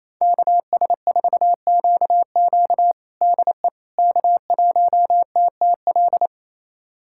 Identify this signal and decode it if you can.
Morse code, heard as a keyed tone, about 28 wpm, 690 Hz. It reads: KS4QQ DE K1TTL
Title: USA Callsign DE Callsign - 5 Characters - 28wpm